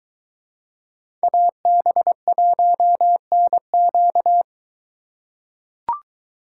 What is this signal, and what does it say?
Morse code, heard as a keyed tone, about 22 wpm, 700 Hz. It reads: AB1NQ